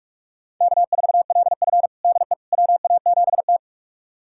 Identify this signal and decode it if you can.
Morse code, heard as a keyed tone, about 42 wpm, 700 Hz. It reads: K4LF DE WA7T
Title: USA Callsign DE Callsign - 4 Characters - 45wpm